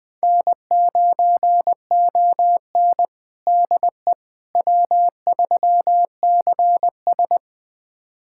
Morse code, heard as a keyed tone, about 20 wpm, 700 Hz. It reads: N9ON DE W3CS
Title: USA Callsign DE Callsign - 4 Characters - 20wpm